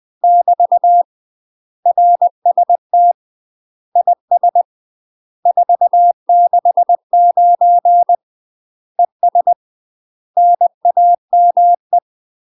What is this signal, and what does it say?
Morse code, heard as a keyed tone, about 20 wpm, 700 Hz. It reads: <BT> RST IS 469 ES NAME
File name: UK MASTER QSOs TRAINING / 20WPM / UK MASTER QSOs 0005